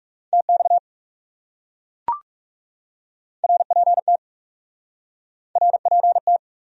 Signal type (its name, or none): Morse code